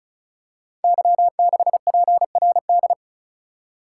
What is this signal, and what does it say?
Morse code, heard as a keyed tone, about 35 wpm, 690 Hz. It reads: Y6PRD
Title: Code Groups - 5 Letter:Numbers - 35wpm